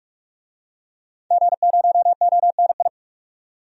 Morse code, heard as a keyed tone, about 44 wpm, 700 Hz. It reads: G0ONI
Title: DXCC-100 Call Signs - 5 Characters - 45wpm